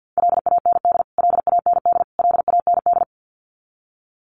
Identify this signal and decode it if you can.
Morse code, heard as a keyed tone, about 52 wpm, 700 Hz. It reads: LAND LAND LAND
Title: Top 300 Words Repeated 50wpm